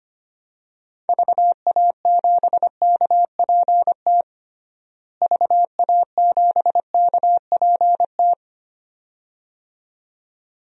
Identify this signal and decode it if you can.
Morse code, heard as a keyed tone, about 25 wpm, 700 Hz. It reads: VA7KPT VA7KPT
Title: DXCC-100 Call Signs - 6 Characters - Repeated 2x - 25wpm